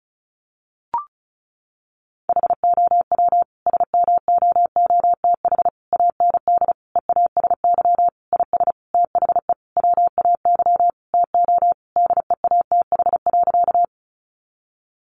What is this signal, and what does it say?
Morse code, heard as a keyed tone, about 34 wpm, 700 Hz. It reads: HOW SMOOTH AND EASY IS THE WAY TO DEATH.
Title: Sentences from Top 1400 Words 35wpm